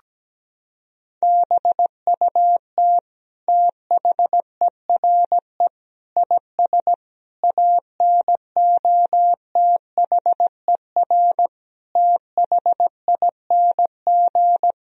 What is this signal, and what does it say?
Morse code, heard as a keyed tone, about 17 wpm, 700 Hz. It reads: BUT THERE IS ANOTHER THING
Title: Sentences from Top 1000 Words - Review Part 4 - 17wpm